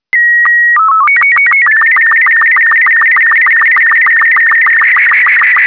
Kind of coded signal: SSTV (slow-scan television)